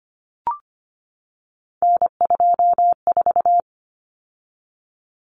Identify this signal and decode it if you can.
Morse code, heard as a keyed tone, about 25 wpm, 700 Hz. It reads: N24
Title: Common Cut Numbers - 3-Digit 25wpm